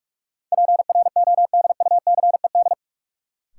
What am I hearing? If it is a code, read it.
Morse code, heard as a keyed tone, about 41 wpm, 700 Hz. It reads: PRODUCED
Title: Sentences from Top 1000 Words - Review Part 6 - New Word Drill - 45wpm